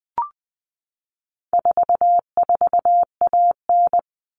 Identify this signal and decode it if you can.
Morse code, heard as a keyed tone, about 20 wpm, 700 Hz. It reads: 44AN
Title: All Cut Numbers - 4-Digit 20wpm